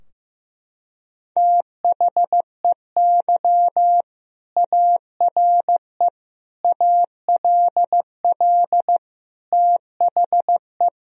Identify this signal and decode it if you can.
Morse code, heard as a keyed tone, about 15 wpm, 700 Hz. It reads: THEY ARE ALL THE
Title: Sentences from Top 1000 Words - Review Part 4 - 15wpm